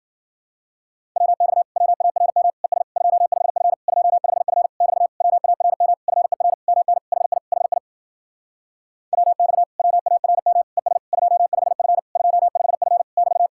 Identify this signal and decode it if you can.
Morse code, heard as a keyed tone, about 53 wpm, 700 Hz. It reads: WX WARM ES 25F 25F <BT> WARM FER MN HI HI WX WARM ES 25F 25F <BT>